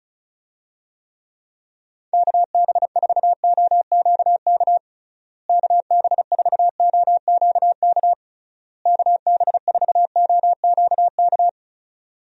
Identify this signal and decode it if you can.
Morse code, heard as a keyed tone, about 34 wpm, 700 Hz. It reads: KB4OQK KB4OQK KB4OQK
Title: USA Call Signs - 6 Characters - Repeated 3x - 35wpm